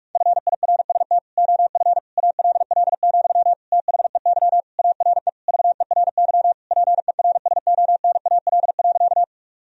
Morse code, heard as a keyed tone, about 46 wpm, 700 Hz. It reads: FIRST OF ALL, THEY ARE VERY PERSONAL.
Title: Sentences from Top 1000 Words - Review Part 4 - 45wpm